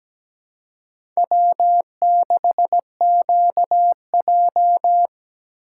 Morse code, heard as a keyed tone, about 17 wpm, 700 Hz. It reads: W6QJ
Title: USA Call Signs - 4 Characters - 2x Character Spacing - 17wpm